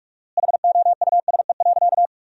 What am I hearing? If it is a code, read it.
Morse code, heard as a keyed tone, about 46 wpm, 700 Hz. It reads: HOUSE.
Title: 4x Word-Spacing - Sentences from Top 500 Words - 45wpm